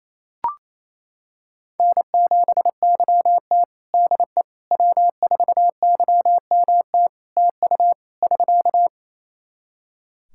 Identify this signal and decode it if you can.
Morse code, heard as a keyed tone, about 27 wpm, 700 Hz. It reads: N7YT DE W4YMT TU <SK>
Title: K1USN SST QSO Practice 28wpm